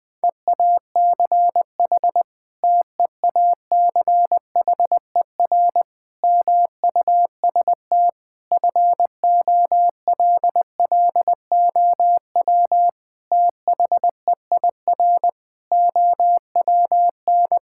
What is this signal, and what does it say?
Morse code, heard as a keyed tone, about 19 wpm, 700 Hz. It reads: EACH TEACHER MUST FOLLOW THEIR OWN
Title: Sentences from Top 1000 Words - Review Part 6 - 20wpm